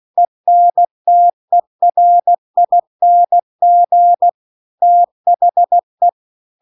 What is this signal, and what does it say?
Morse code, heard as a keyed tone, about 16 wpm, 700 Hz. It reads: ENTERING THE